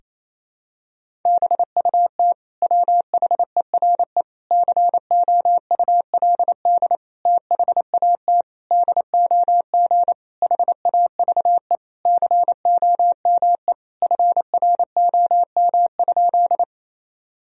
Morse code, heard as a keyed tone, about 27 wpm, 700 Hz. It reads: BUT WHERE COULD THAT DOG HAVE COME FROM?
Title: Sentences from Top 1000 Words - Review Part 2 - 28wpm